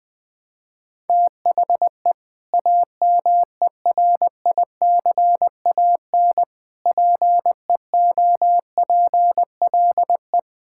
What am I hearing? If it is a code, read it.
Morse code, heard as a keyed tone, about 20 wpm, 700 Hz. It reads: THE AMERICAN PEOPLE
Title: Sentences with 3-Word Phrases - 20wpm